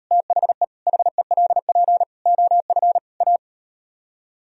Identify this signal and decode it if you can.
Morse code, heard as a keyed tone, about 38 wpm, 700 Hz. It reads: THE HELP OF A